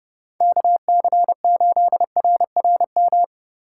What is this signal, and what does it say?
Morse code, heard as a keyed tone, about 30 wpm, 700 Hz. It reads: KC8RRM
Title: USA Call Signs - 6 Characters - 2x Character Spacing - 30wpm